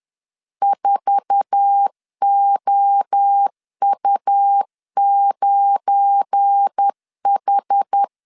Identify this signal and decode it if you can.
Morse code, heard as a keyed tone, about 10 wpm, 800 Hz. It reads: 4OU9H
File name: LEZIONI / 8 YJ45 / 8 YJ4 10wpm 2633